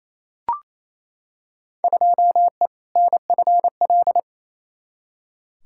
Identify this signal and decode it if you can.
Morse code, heard as a keyed tone, about 28 wpm, 700 Hz. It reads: 2E NFL
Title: Field Day Exchanges Encore - 28wpm